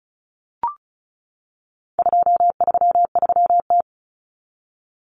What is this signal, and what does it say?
Morse code, heard as a keyed tone, about 34 wpm, 700 Hz. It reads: 233T
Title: Common Cut Numbers - 4-Digit 35wpm